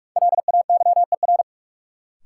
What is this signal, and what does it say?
Morse code, heard as a keyed tone, about 42 wpm, 700 Hz. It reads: LAYER
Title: Sentences from Top 1600 Words - New Word Drill - 45wpm